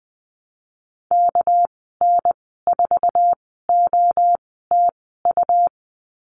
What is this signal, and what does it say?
Morse code, heard as a keyed tone, about 20 wpm, spaced as for 14 wpm, 700 Hz. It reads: KN4OTU
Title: 238 Lesson 38g - Letter q - Call Signs 20wpm